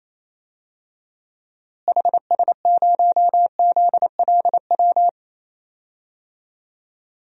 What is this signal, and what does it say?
Morse code, heard as a keyed tone, about 28 wpm, 700 Hz. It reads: HS0ZLW
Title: DXCC-100 Call Signs - 6 Characters - Repeated 2x - 28wpm